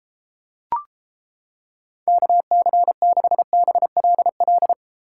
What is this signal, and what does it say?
Morse code, heard as a keyed tone, about 31 wpm, 700 Hz. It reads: KC6BLL